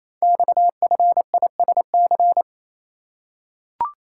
Morse code, heard as a keyed tone, about 28 wpm, 700 Hz. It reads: XFISC
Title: Code Groups - 5 Letters - 28wpm